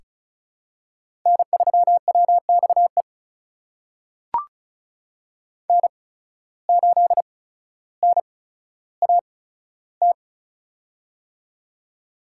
Morse code, heard as a keyed tone, about 36 wpm, spaced as for 15 wpm, 700 Hz. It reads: N3WXE N8NAT
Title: USA Call Signs - 8x Character Spacing - 35wpm